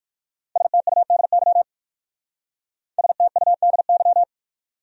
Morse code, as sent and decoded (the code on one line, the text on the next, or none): ... - ..- -.. -.-- / ... - ..- -.. -.--
STUDY STUDY